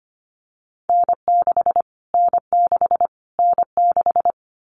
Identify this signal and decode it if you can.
Morse code, heard as a keyed tone, about 25 wpm, 700 Hz. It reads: N6 N6 N6
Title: Callsign Prefixes for Top 25 DXCC - Repeated 3x - 25wpm